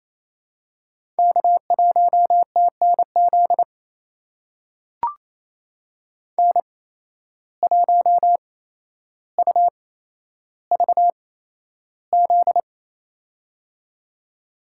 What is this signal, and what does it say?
Morse code, heard as a keyed tone, about 28 wpm, spaced as for 12 wpm, 700 Hz. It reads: K1TNZ N1UVZ
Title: USA Call Signs - 5 Characters - 8x Character Spacing - 28wpm